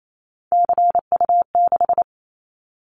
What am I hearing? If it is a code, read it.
Morse code, heard as a keyed tone, about 28 wpm, 700 Hz. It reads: CU6
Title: Callsign Prefixes for Top 50 DXCC - 28wpm